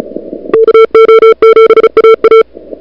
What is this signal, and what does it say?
Morse code, heard as a keyed tone, about 35 wpm, 440 Hz. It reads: KO7AA
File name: n6tr / Fab2007CWSprintHour1